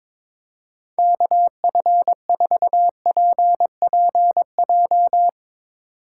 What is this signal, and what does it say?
Morse code, heard as a keyed tone, about 22 wpm, 700 Hz. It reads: KF4PPJ
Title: USA Call Signs - 6 Characters - 2x Character Spacing - 22wpm